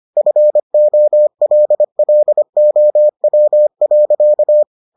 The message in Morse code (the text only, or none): FOLLOW.